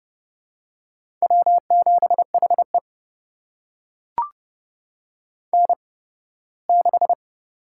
Morse code, heard as a keyed tone, about 30 wpm, spaced as for 18 wpm, 700 Hz. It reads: W7HE N6
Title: USA Call Signs - 4 Characters - 8x Character Spacing - 30wpm